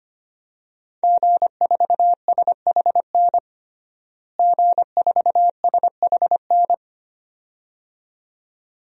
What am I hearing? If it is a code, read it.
Morse code, heard as a keyed tone, about 25 wpm, 700 Hz. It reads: G4SHN G4SHN
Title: DXCC-100 Call Signs - 5 Characters - Repeated 2x - 25wpm